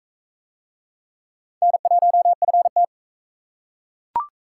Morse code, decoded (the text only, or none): N1FT